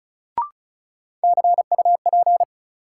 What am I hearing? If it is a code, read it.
Morse code, heard as a keyed tone, about 36 wpm, 700 Hz. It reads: CUP
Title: Sentences from Top 1600 Words - New Word Drill Alt - 35wpm